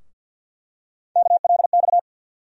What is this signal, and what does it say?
Morse code, heard as a keyed tone, about 49 wpm, 700 Hz. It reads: KLX
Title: USA Call Sign Postfixes - 3 Letters 50wpm